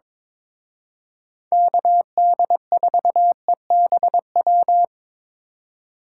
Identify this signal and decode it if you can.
Morse code, heard as a keyed tone, about 22 wpm, 700 Hz. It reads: KD4EBW
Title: Speed Racing - USA 2x3 Call Signs 22wpm